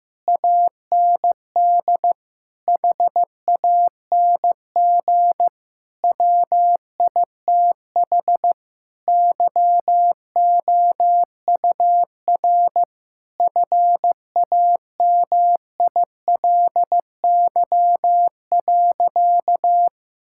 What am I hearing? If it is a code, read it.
Morse code, heard as a keyed tone, about 15 wpm, 700 Hz. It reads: AND HANG WITH YOUR FAMILY.
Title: Sentences from Top 1600 Words 15wpm